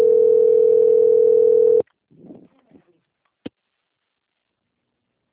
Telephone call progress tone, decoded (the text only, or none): ringback tone